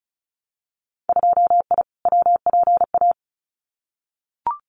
Morse code, heard as a keyed tone, about 34 wpm, 700 Hz. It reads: 2I WPA
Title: Winter Field Day Exchanges 35wpm